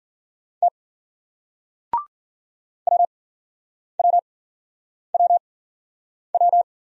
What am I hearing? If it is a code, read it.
Morse code, heard as a keyed tone, about 48 wpm, spaced as for 9 wpm, 700 Hz. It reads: T WWWW
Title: Speed-Racing Single Letters 40wpm